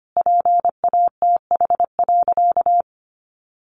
Morse code, heard as a keyed tone, about 25 wpm, 700 Hz. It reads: PATH.
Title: Sentences from Top 1400 Words 25wpm